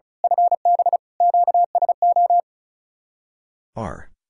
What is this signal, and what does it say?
Morse code, heard as a keyed tone, about 33 wpm, 700 Hz. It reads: FB QSO
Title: Speed-Racing Common QSO Phrases Encore - 35wpm